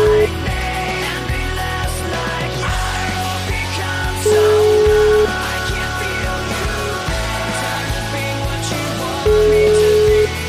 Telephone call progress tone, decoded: ringback tone

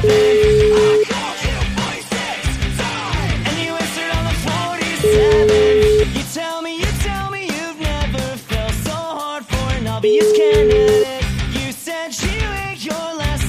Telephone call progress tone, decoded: ringback tone

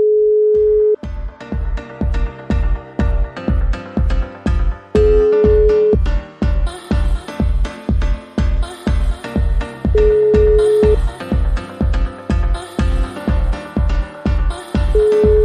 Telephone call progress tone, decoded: ringback tone

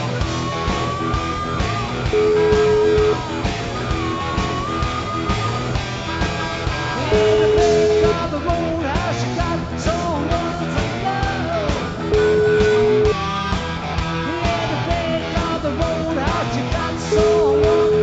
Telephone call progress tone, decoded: ringback tone

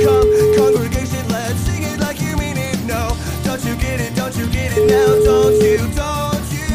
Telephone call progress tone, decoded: ringback tone

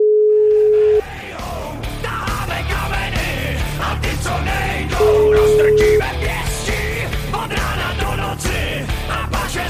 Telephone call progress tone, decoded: ringback tone

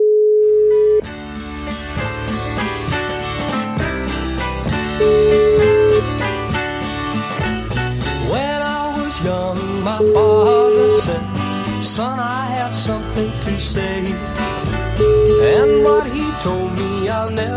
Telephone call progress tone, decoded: ringback tone

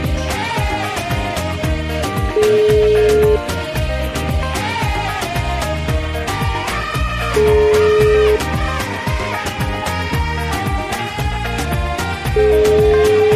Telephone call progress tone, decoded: ringback tone

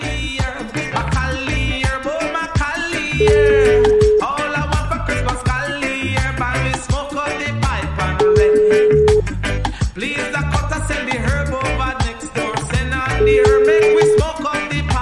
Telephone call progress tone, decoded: ringback tone

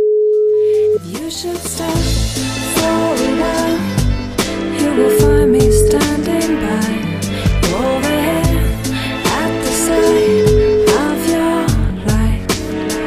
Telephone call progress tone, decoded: ringback tone

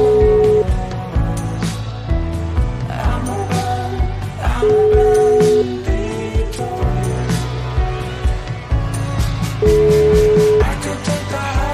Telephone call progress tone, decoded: ringback tone

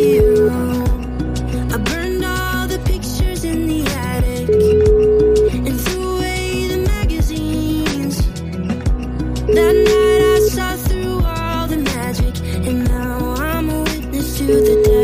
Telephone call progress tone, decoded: ringback tone

